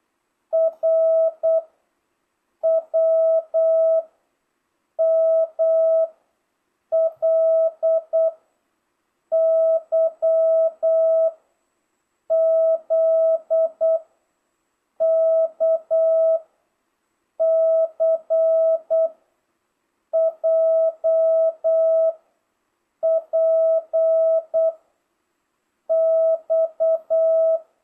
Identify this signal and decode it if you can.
Morse code, heard as a keyed tone, about 8 wpm, 650 Hz. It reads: R W M L Y Z K C J P X